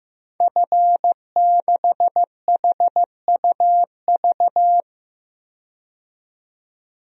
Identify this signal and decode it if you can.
Morse code, heard as a keyed tone, about 16 wpm, 700 Hz. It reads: F6HUV
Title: DXCC-100 Call Signs - 5 Characters - 15wpm